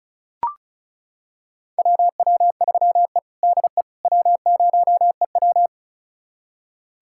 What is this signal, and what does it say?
Morse code, heard as a keyed tone, about 36 wpm, 700 Hz. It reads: WW3E DE W0EW
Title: USA Callsign DE Callsign - 4 Characters - 35wpm